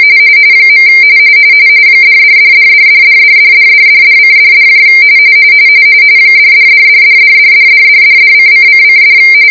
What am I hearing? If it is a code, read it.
RTTY, about 45.45 baud. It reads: . $$$$$$$$$$$$$:$$ $$$$$$$$$$$$::::$